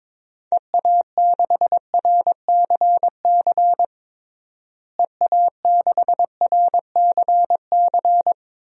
Morse code, heard as a keyed tone, about 22 wpm, 700 Hz. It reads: EA6RCC EA6RCC
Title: DXCC-100 Call Signs - 6 Characters - Repeated 3x - 22wpm